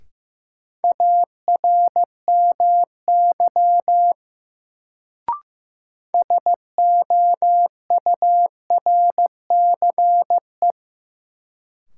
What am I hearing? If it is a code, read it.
Morse code, heard as a keyed tone, about 16 wpm, 700 Hz. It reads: ARMY SOURCE
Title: Sentences from Top 800 Words - New Word Drill - 15wpm